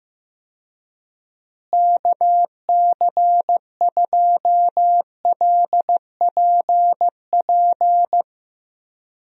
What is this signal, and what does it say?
Morse code, heard as a keyed tone, about 15 wpm, 700 Hz. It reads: KC2LPP
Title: USA Call Signs - 6 Characters - Repeated 3x - 15wpm